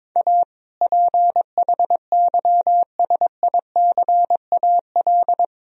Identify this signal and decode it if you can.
Morse code, heard as a keyed tone, about 22 wpm, 700 Hz. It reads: A PHYSICAL
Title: Sentences from Top 1000 Words - Review Part 4 - 22wpm